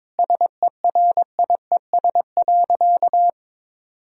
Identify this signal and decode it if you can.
Morse code, heard as a keyed tone, about 22 wpm, 700 Hz. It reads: SERIES.
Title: Sentences from Top 1000 Words - Review Part 5 - 22wpm